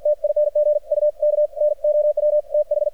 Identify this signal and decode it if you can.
Morse code, heard as a keyed tone, about 53 wpm, 600 Hz. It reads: TING U KNOW TH
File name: archived / CRSnet